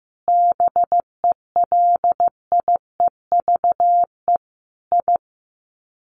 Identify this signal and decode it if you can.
Morse code, heard as a keyed tone, about 15 wpm, 700 Hz. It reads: BELIEVE I
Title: Sets of 5 Words Encore2 15wpm